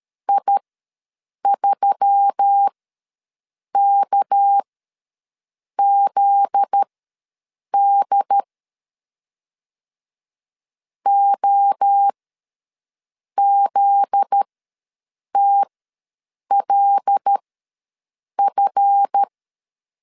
Morse code, heard as a keyed tone, about 13 wpm, spaced as for 6 wpm, 800 Hz. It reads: I3KZD OZTLF
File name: LEZIONI / 7 GF9Z / 7 GF9Z ws 12wpm 3321